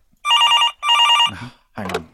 There is no code.